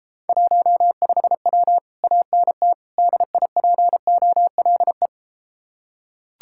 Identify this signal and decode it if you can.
Morse code, heard as a keyed tone, about 33 wpm, 700 Hz. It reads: 15W ANT DIPOLE